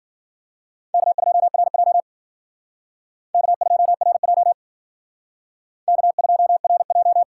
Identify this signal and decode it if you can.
Morse code, heard as a keyed tone, about 48 wpm, 700 Hz. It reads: K2RJ K2RJ K2RJ
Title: Speed Racing - USA 1x2 Call Signs 40wpm